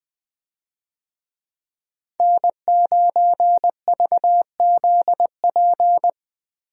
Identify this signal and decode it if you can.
Morse code, heard as a keyed tone, about 20 wpm, 700 Hz. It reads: N9VZP
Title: USA Call Signs - 5 Characters - Repeated 2x - 20wpm